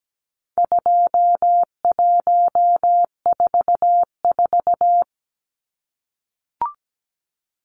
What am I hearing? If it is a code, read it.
Morse code, heard as a keyed tone, about 17 wpm, 700 Hz. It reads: 2144
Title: Numbers - 4-Digit 17wpm